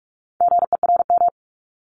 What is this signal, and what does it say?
Morse code, heard as a keyed tone, about 46 wpm, 700 Hz. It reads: GERM
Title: Multi-letter Phonemes 45wpm